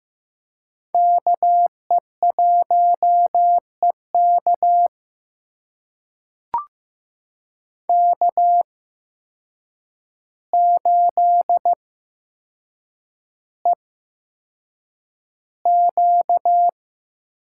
Morse code, heard as a keyed tone, about 15 wpm, spaced as for 7 wpm, 700 Hz. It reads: KE1EK K8EQ
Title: USA Call Signs - 5 Characters - 8x Character Spacing - 15wpm